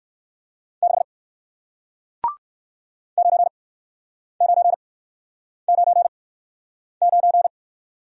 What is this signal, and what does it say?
Morse code, heard as a keyed tone, about 58 wpm, spaced as for 9 wpm, 700 Hz. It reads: 6 9999